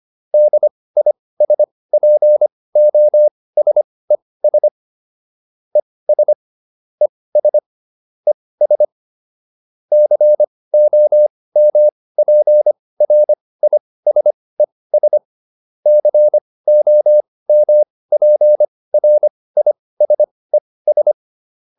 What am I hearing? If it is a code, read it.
Morse code, heard as a keyed tone, about 25 wpm, spaced as for 18 wpm, 600 Hz. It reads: DISPOSES ES ES ES COMPRISES COMPRISES